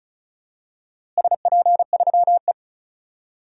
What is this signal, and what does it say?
Morse code, heard as a keyed tone, about 34 wpm, 700 Hz. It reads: SP3E
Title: DXCC-100 Call Signs - 4 Characters - Repeated 3x - 35wpm